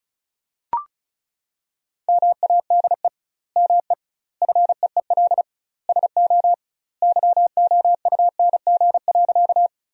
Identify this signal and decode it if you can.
Morse code, heard as a keyed tone, about 36 wpm, 700 Hz. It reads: MADE ME FEEL SO YOUNG.
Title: 2x Word-Spacing - Sentences from Top 400 Words - 35wpm